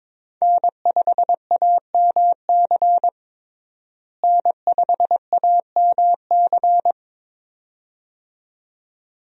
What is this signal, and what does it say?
Morse code, heard as a keyed tone, about 22 wpm, 700 Hz. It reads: N5AMC N5AMC
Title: USA Call Signs - 5 Characters - Repeated 2x - 22wpm